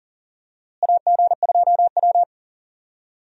Morse code, heard as a keyed tone, about 38 wpm, 700 Hz. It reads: AG2W
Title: USA Call Signs - 4 Characters - Repeated 3x - 40wpm